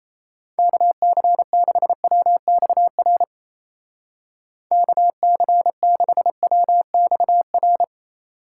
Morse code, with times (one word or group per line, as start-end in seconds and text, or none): 0.6-3.2 KC6WXR
4.7-7.8 KC6WXR